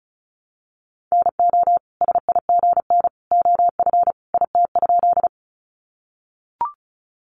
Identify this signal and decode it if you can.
Morse code, heard as a keyed tone, about 34 wpm, 700 Hz. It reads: NO SIGN OF IT?
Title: Sentences from Top 1000 Words 35wpm